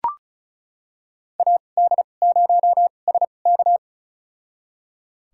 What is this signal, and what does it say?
Morse code, heard as a keyed tone, about 36 wpm, spaced as for 25 wpm, 700 Hz. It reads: AD0SK
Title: USA Call Signs - 5 Characters - 2x Character Spacing - 35wpm